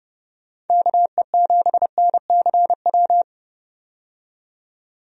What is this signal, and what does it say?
Morse code, heard as a keyed tone, about 30 wpm, 700 Hz. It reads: KE7NCW
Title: USA Call Signs - 6 Characters - 30wpm